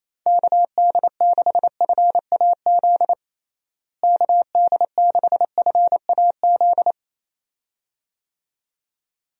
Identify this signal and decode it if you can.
Morse code, heard as a keyed tone, about 28 wpm, 700 Hz. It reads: KD6FAZ KD6FAZ
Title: USA Call Signs - 6 Characters - Repeated 2x - 28wpm